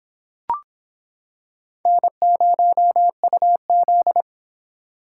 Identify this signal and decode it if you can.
Morse code, heard as a keyed tone, about 25 wpm, 700 Hz. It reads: N0UZ